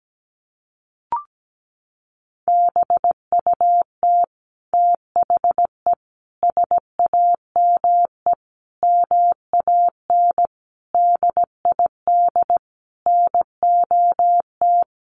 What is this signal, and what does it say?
Morse code, heard as a keyed tone, about 17 wpm, 700 Hz. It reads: BUT THE SAME MAN DID NOT
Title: A Sound Of Thunder 17wpm